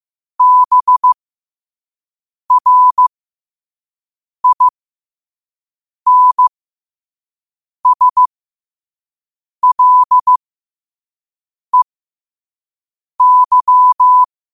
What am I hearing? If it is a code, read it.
Morse code, heard as a keyed tone, about 15 wpm, spaced as for 5 wpm, 1000 Hz. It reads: BRINSLEY
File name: qotdfiles / 5wpm